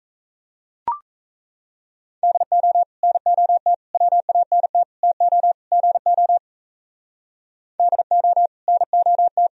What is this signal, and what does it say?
Morse code, heard as a keyed tone, about 35 wpm, 700 Hz. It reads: DO NOT WANT TO GO DO NOT
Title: Speed-Racing - Sets of 5 Words 28wpm